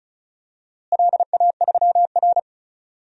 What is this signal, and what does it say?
Morse code, heard as a keyed tone, about 36 wpm, 700 Hz. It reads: LA3R
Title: DXCC-100 Call Signs - 4 Characters - Repeated 3x - 35wpm